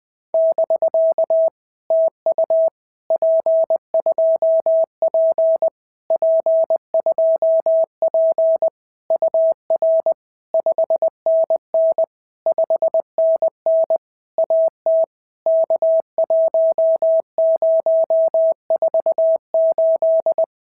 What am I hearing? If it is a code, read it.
Morse code, heard as a keyed tone, about 20 wpm, 650 Hz. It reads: <BK> TU P2P P2P UR 5NN 5NN AT K1048
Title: POTA QSO Practice - 20wpm